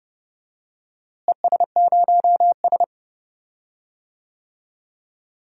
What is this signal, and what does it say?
Morse code, heard as a keyed tone, about 29 wpm, 700 Hz. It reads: ES0S
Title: DXCC-100 Call Signs - 4 Characters - Repeated 3x - 30wpm